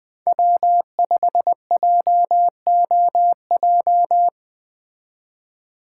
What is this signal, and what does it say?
Morse code, heard as a keyed tone, about 20 wpm, 700 Hz. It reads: W5JOJ
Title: Speed Racing - USA 1x3 Call Signs 17wpm